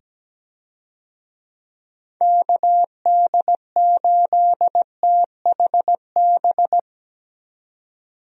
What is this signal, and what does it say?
Morse code, heard as a keyed tone, about 17 wpm, 700 Hz. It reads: KD8THB
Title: USA Call Signs - 6 Characters - 17wpm